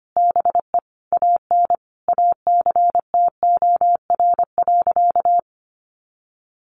Morse code, heard as a keyed tone, about 25 wpm, 700 Hz. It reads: BE AN ACTOR.
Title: Sentences from Top 1500 Words 25wpm